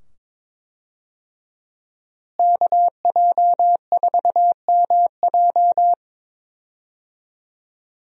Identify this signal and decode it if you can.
Morse code, heard as a keyed tone, about 22 wpm, 700 Hz. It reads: KJ4MJ
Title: USA Call Signs - 5 Characters - Repeated 2x - 22wpm